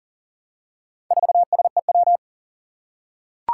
Morse code, heard as a keyed tone, about 40 wpm, 700 Hz. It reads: 4SEW
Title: Four Challenging Letter-Number Combinations 40wpm